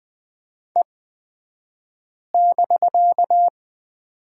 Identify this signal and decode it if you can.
Morse code, heard as a keyed tone, about 20 wpm, spaced as for 5 wpm, 700 Hz. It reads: E<BK>
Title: 249 Lesson 40d - BK - Two Characters 20wpm